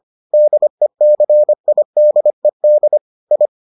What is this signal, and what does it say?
Morse code, heard as a keyed tone, about 25 wpm, 600 Hz. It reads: DECIDED I